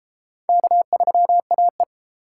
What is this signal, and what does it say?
Morse code, heard as a keyed tone, about 32 wpm, 700 Hz. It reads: K3AE